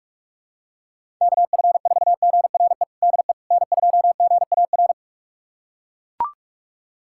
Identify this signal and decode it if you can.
Morse code, heard as a keyed tone, about 41 wpm, 700 Hz. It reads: KF4GRE DE N2GAR
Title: USA Callsign DE Callsign 45wpm